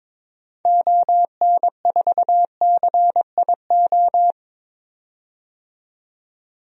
Morse code, heard as a keyed tone, about 21 wpm, 700 Hz. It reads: ON4CIO
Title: DXCC-100 Call Signs - 6 Characters - 22wpm